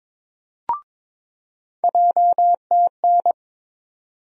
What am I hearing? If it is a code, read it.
Morse code, heard as a keyed tone, about 22 wpm, 700 Hz. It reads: JTN
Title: USA Call Sign Postfixes - 3 Letters 22wpm